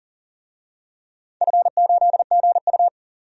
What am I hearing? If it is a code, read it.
Morse code, heard as a keyed tone, about 39 wpm, 700 Hz. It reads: F8GU
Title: DXCC-100 Call Signs - 4 Characters - 40wpm